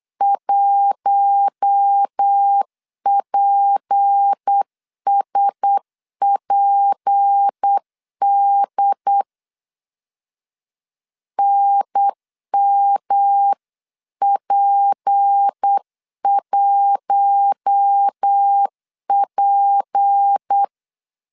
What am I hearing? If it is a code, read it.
Morse code, heard as a keyed tone, about 8 wpm, 800 Hz. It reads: 1PSPD NMP1P
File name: LEZIONI / 6 PKX0 / 6 P 8wpm 3025